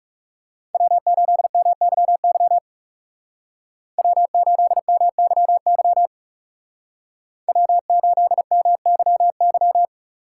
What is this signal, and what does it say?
Morse code, heard as a keyed tone, about 38 wpm, 700 Hz. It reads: W8MYY W8MYY W8MYY